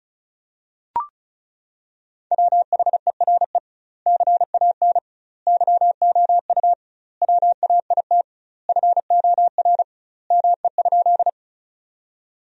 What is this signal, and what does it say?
Morse code, heard as a keyed tone, about 36 wpm, 700 Hz. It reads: WHERE CAN YOU WAIT FOR ME?
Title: 2x Word-Spacing - Sentences from Top 500 Words - 35wpm